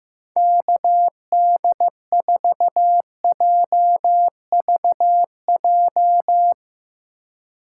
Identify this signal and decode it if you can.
Morse code, heard as a keyed tone, about 15 wpm, 700 Hz. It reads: KD4JVJ
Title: Speed Racing - USA 2x3 Call Signs 15wpm